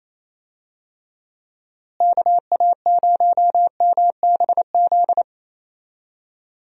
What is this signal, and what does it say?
Morse code, heard as a keyed tone, about 27 wpm, 700 Hz. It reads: KA0MBZ
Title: USA Call Signs - 6 Characters - Repeated 2x - 28wpm